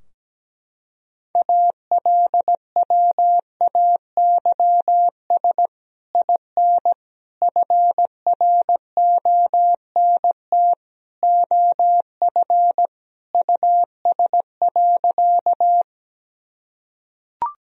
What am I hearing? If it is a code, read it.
Morse code, heard as a keyed tone, about 17 wpm, 700 Hz. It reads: ALWAYS IN FRONT OF US.
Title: Sentences from Top 1000 Words - Review Part 2 - 17wpm